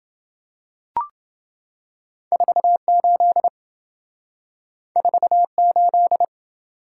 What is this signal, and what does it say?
Morse code, heard as a keyed tone, about 26 wpm, 700 Hz. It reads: 48 48